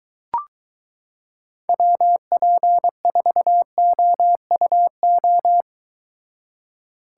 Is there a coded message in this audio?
Yes: Morse code from 1.7 to 5.6 s, heard as a keyed tone, about 23 wpm, 700 Hz. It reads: WP4OUO